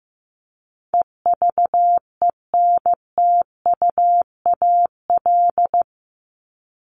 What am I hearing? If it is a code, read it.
Morse code, heard as a keyed tone, about 15 wpm, 700 Hz. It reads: EVENTUAL
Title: LY Suffix - Top 100 Words with Base Words 15wpm